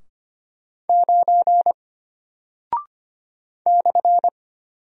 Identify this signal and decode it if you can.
Morse code, heard as a keyed tone, about 25 wpm, 700 Hz. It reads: 9 /
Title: Single Character (Letter:Number:Punctuation:Prosign) Alt 25wpm